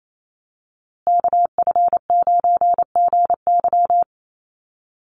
Morse code, heard as a keyed tone, about 28 wpm, 700 Hz. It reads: KF9GY
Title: USA Call Signs Repeated 2x 28wpm